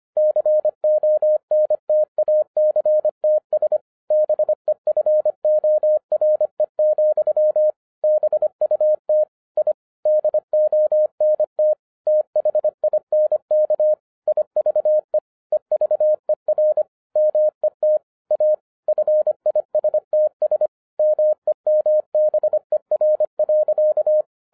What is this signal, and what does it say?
Morse code, heard as a keyed tone, about 25 wpm, 610 Hz. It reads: CONTACTS BEFORE, BUT I DONT THINK IVE EVER MET A FISTS MEMBER.
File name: 2012-01-mp3 / n06-FIRST FISTS CONTACT-25wpm